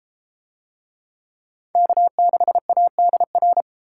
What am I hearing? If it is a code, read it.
Morse code, heard as a keyed tone, about 31 wpm, 700 Hz. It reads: K6ADR